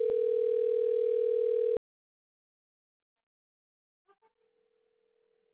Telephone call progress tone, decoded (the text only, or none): ringback tone